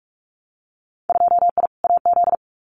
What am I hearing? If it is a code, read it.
Morse code, heard as a keyed tone, about 44 wpm, 700 Hz. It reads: 2I AZ
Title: Winter Field Day Exchanges 45wpm